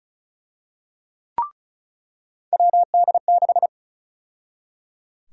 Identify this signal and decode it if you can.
Morse code, heard as a keyed tone, about 34 wpm, 700 Hz. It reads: WD6
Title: USA Call Sign Prefixes - 35wpm